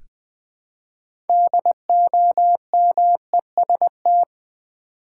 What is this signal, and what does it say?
Morse code, heard as a keyed tone, about 20 wpm, 700 Hz. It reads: DOMEST
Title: Build A Word - 3 Letters At A Time - 8-Letter Words - 20wpm